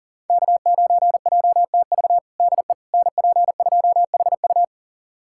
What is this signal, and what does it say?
Morse code, heard as a keyed tone, about 40 wpm, 700 Hz. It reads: K9JTV DE NP2HU
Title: USA Callsign DE Callsign - 5 Characters - 40wpm